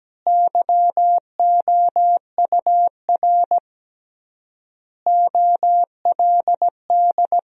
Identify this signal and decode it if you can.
Morse code, heard as a keyed tone, about 17 wpm, 700 Hz. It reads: YOUR OLD
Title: 3x Word-Spacing - Sentences from Top 500 Words - 17wpm